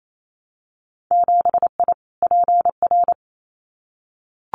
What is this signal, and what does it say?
Morse code, heard as a keyed tone, about 28 wpm, 700 Hz. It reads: 7I PR
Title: Winter Field Day Exchanges Encore 28wpm